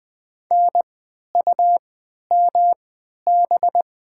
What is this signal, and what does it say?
Morse code, heard as a keyed tone, about 20 wpm, spaced as for 11 wpm, 700 Hz. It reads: NUMB
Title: 3x Character Spacing - Top 100 Words - 20wpm